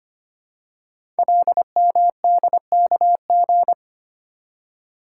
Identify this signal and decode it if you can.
Morse code, heard as a keyed tone, about 25 wpm, 700 Hz. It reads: LMDKG
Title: Code Groups - 5 Letters - 25wpm